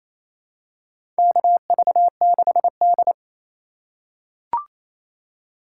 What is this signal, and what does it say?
Morse code, heard as a keyed tone, about 28 wpm, 700 Hz. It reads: KV6D
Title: USA Call Signs - 3x Character Spacing - 28wpm